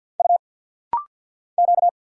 Morse code, heard as a keyed tone, about 50 wpm, 700 Hz. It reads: U Q
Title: Single Letters - Rapid-Fire 50wpm